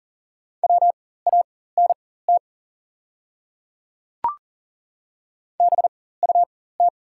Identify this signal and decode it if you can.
Morse code, heard as a keyed tone, about 40 wpm, spaced as for 19 wpm, 700 Hz. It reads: WANT BUT
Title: 4x Character Spacing - Top 500 Words - 40wpm